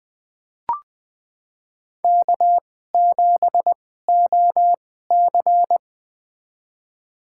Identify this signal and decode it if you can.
Morse code, heard as a keyed tone, about 20 wpm, spaced as for 14 wpm, 700 Hz. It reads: K7OC
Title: USA Call Signs - 4 Characters - 2x Character Spacing - 20wpm